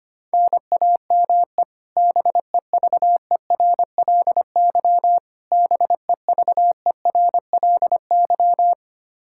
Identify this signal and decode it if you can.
Morse code, heard as a keyed tone, about 25 wpm, 700 Hz. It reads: NAME BEVERLY BEVERLY
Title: Practice QSO - Exchange 1 - Beginning - 25wpm